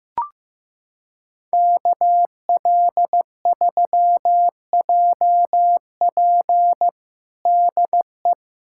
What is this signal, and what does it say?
Morse code, heard as a keyed tone, about 15 wpm, 700 Hz. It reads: KL3JP DE
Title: USA Callsign DE Callsign - 5 Characters - 15wpm